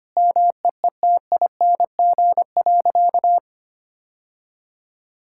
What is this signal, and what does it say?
Morse code, heard as a keyed tone, about 25 wpm, 700 Hz. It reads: MEETING.
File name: practice / 25wpm / Sentences with 3-Word Phrases - 25wpm